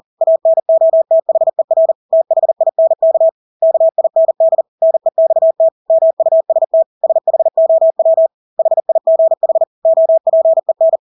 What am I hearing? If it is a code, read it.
Morse code, heard as a keyed tone, about 40 wpm, 650 Hz. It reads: ANOTHER THINK KIND NEXT MUST SHOW HIGH OPEN
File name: Engelske ord 40-40 wpm